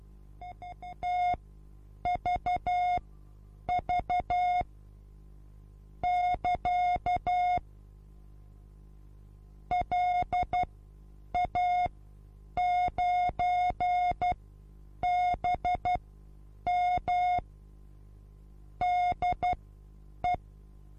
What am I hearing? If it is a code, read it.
Morse code, heard as a keyed tone, about 12 wpm, spaced as for 8 wpm, 720 Hz. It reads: TVV <KA> LA9BM DE
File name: Onlinelehrgang / morsen / DJ4UF Morsen23